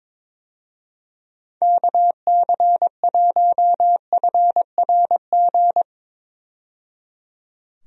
Morse code, heard as a keyed tone, about 21 wpm, 700 Hz. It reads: KC1FRG
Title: USA Call Signs - 6 Characters - 22wpm